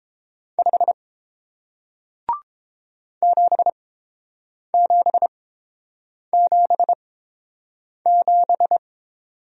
Morse code, heard as a keyed tone, about 29 wpm, spaced as for 7 wpm, 700 Hz. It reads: 5 7777